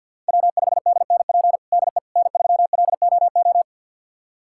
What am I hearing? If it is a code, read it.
Morse code, heard as a keyed tone, about 49 wpm, 700 Hz. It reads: W5DNP DE N3LOO
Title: USA Callsign DE Callsign - 5 Characters - 50wpm